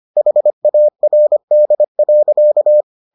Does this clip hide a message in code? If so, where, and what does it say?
Yes: Morse code from 0.2 to 2.8 s, heard as a keyed tone, about 25 wpm, 600 Hz. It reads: HARD.